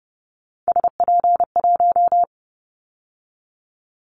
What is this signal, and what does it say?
Morse code, heard as a keyed tone, about 30 wpm, 700 Hz. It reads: SP1
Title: Callsign Prefixes for Top 25 DXCC - 30wpm